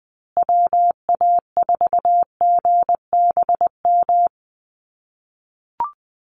Morse code, heard as a keyed tone, about 20 wpm, 700 Hz. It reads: WA4GBM
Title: USA Call Signs Repeated 3x 20wpm